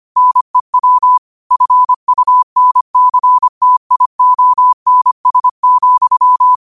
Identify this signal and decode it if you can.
Morse code, heard as a keyed tone, about 25 wpm, 1000 Hz. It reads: NEW FUNCTIONS,